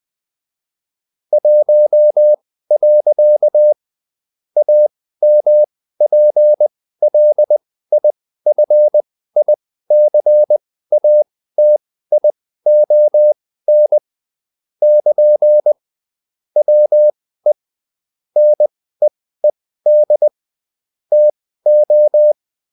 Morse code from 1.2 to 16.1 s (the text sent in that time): 1. AMPLIFICATION <KN>